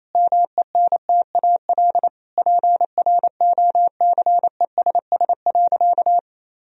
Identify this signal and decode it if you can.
Morse code, heard as a keyed tone, about 28 wpm, 700 Hz. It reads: MENTAL PROCESS.
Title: Sentences from Top 1000 Words - Review Part 4 - 28wpm